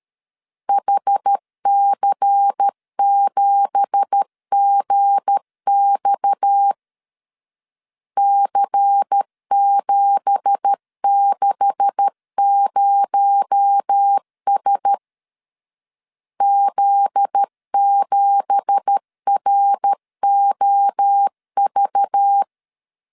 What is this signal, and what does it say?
Morse code, heard as a keyed tone, about 13 wpm, 800 Hz. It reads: HC7GX C760S Z7ROV